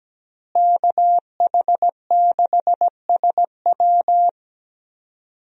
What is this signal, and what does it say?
Morse code, heard as a keyed tone, about 17 wpm, 700 Hz. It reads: KH6SW
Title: DXCC-100 Call Signs - 5 Characters - Repeated 2x - 17wpm